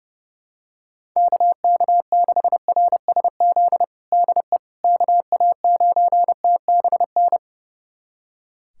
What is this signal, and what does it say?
Morse code, heard as a keyed tone, about 32 wpm, 700 Hz. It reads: KK6RSZ DE KA9TBN
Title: USA Callsign DE Callsign 30wpm